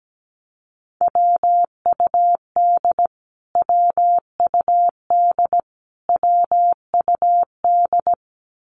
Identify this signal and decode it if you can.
Morse code, heard as a keyed tone, about 17 wpm, 700 Hz. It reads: WUD WUD WUD
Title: Abbreviations Repeated 3x - 17wpm